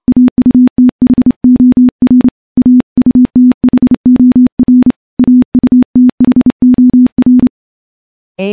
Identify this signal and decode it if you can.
Morse code, heard as a keyed tone, about 29 wpm, 250 Hz. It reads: AUTHOR AUTHOR AUTHOR
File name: morse / morse-training-30WPM